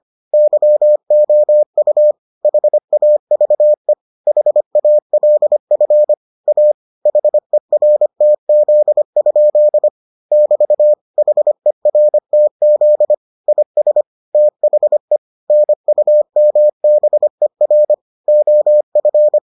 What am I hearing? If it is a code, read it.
Morse code, heard as a keyed tone, about 25 wpm, 600 Hz. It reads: YOU HAVE HALF A HERTZ? <BT> HERTZ IS THE NUMBER OF